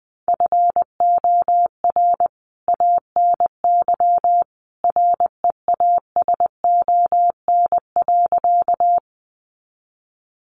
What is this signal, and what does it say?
Morse code, heard as a keyed tone, about 20 wpm, 700 Hz. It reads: FOR ANY REASON.
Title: A Sound Of Thunder 20wpm